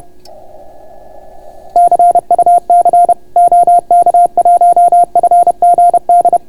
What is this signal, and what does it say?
Morse code, heard as a keyed tone, about 31 wpm, 690 Hz. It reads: CUC OK1FGD